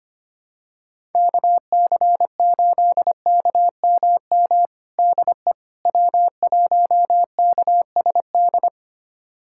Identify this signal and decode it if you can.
Morse code, heard as a keyed tone, about 25 wpm, 700 Hz. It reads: KC8KMM DE W1KSD
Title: USA Callsign DE Callsign 25wpm